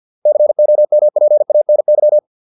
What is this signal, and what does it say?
Morse code, heard as a keyed tone, about 50 wpm, 600 Hz. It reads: COMPANY